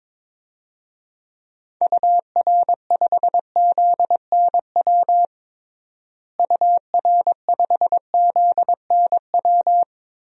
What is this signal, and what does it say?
Morse code, heard as a keyed tone, about 22 wpm, 700 Hz. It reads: UR5ZNW UR5ZNW
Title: DXCC-100 Call Signs - 6 Characters - Repeated 3x - 22wpm